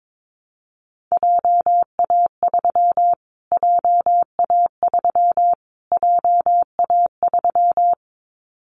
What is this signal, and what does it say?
Morse code, heard as a keyed tone, about 22 wpm, 700 Hz. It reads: JA3 JA3 JA3
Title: Callsign Prefixes for Top 25 DXCC - Repeated 3x - 22wpm